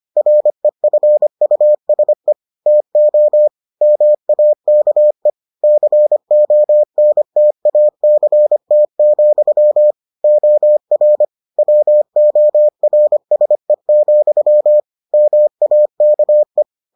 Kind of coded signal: Morse code

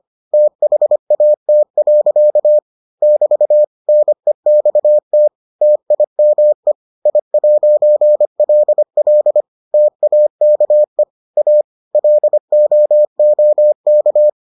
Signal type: Morse code

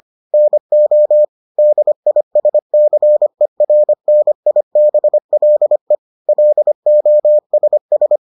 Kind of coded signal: Morse code

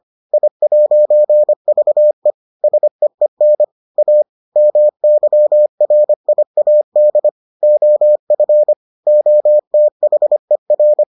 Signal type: Morse code

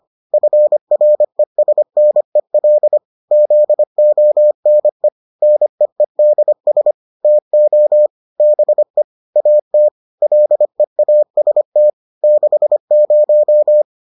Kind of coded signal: Morse code

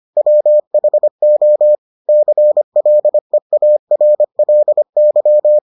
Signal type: Morse code

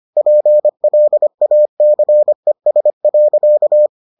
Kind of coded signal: Morse code

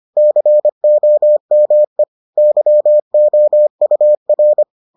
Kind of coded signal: Morse code

